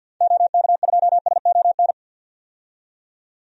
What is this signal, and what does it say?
Morse code, heard as a keyed tone, about 52 wpm, 700 Hz. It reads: OK2SON
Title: DXCC-100 Call Signs - 6 Characters - 50wpm